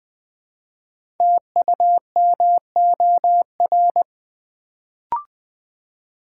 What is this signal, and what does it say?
Morse code, heard as a keyed tone, about 20 wpm, 700 Hz. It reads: TUMOR
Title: Build A Word - 1 Letter At A Time - 3 to 5-Letter Words - 20wpm